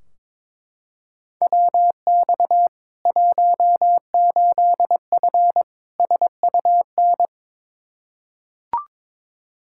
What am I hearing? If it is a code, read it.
Morse code, heard as a keyed tone, about 22 wpm, 700 Hz. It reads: WX 18F SUN
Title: Practice QSO - Exchange 2 - Info - 22wpm